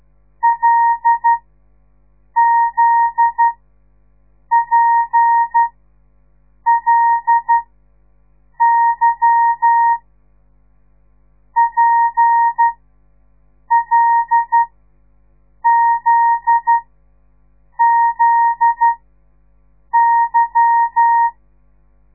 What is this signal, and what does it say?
Morse code, heard as a keyed tone, about 12 wpm, spaced as for 6 wpm, 930 Hz. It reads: LZPLY PLZZY
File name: radiomorningstar / morse / les 8